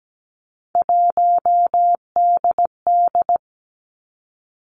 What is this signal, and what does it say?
Morse code, heard as a keyed tone, about 17 wpm, 700 Hz. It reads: 1DD
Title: Three Challenging Letter-Number Combinations 17wpm